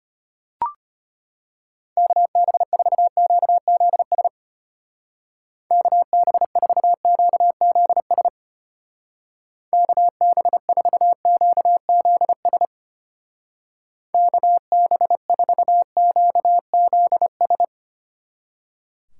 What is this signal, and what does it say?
Morse code, heard as a keyed tone, about 33 wpm, 700 Hz. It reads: KB4QZS KB4QZS KB4QZS KB4QZS